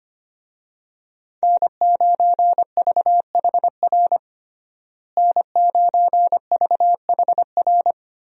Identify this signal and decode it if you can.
Morse code, heard as a keyed tone, about 25 wpm, 700 Hz. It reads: N9VHR N9VHR
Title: USA Call Signs - 5 Characters - Repeated 3x - 25wpm